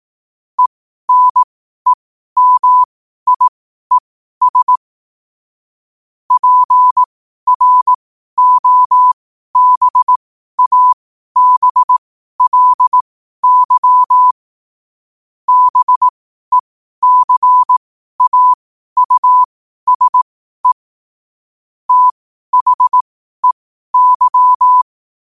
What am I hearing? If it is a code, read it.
Morse code, heard as a keyed tone, about 18 wpm, spaced as for 12 wpm, 1000 Hz. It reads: ENEMIES PROBABLY BECAUSE THEY
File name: qotdfiles / 12wpm